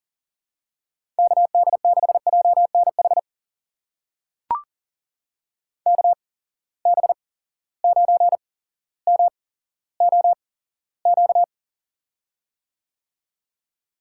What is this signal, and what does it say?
Morse code, heard as a keyed tone, about 38 wpm, spaced as for 17 wpm, 700 Hz. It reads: KD6JNH KB9MOQ
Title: USA Call Signs - 8x Character Spacing - 40wpm